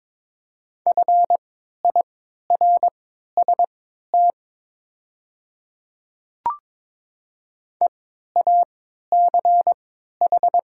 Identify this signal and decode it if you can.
Morse code, heard as a keyed tone, about 22 wpm, spaced as for 12 wpm, 700 Hz. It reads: FIRST EACH
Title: 3x Character Spacing - Top 300 Words - 22wpm